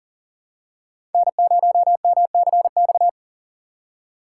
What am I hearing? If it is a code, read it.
Morse code, heard as a keyed tone, about 38 wpm, 700 Hz. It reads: N0MCX
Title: USA Call Signs - 5 Characters - 2x Character Spacing - 40wpm